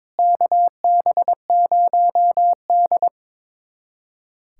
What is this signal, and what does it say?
Morse code, heard as a keyed tone, about 21 wpm, 700 Hz. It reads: KB0D
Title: USA Callsign DE Callsign 22wpm